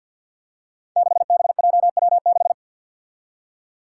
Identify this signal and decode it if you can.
Morse code, heard as a keyed tone, about 52 wpm, 700 Hz. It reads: 6BJW6
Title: Five Challenging Letter-Number Combinations 50wpm